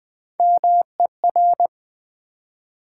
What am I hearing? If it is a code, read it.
Morse code, heard as a keyed tone, about 20 wpm, 700 Hz. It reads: MER
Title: Build A Word - 1 Letter At A Time - 4-Letter Words - 20wpm